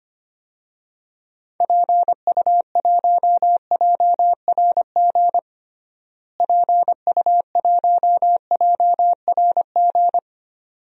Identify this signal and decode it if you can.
Morse code, heard as a keyed tone, about 25 wpm, 700 Hz. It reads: PU1JRG PU1JRG
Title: DXCC-100 Call Signs - 6 Characters - Repeated 2x - 25wpm